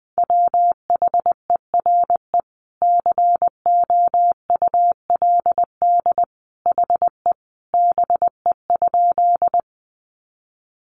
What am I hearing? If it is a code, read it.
Morse code, heard as a keyed tone, about 20 wpm, 700 Hz. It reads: WHERE COULD HE BE?
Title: Sentences from Top 200 Words 20wpm